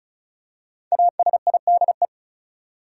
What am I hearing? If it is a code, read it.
Morse code, heard as a keyed tone, about 36 wpm, 700 Hz. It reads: ASIDE
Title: Five Letter Words - 35wpm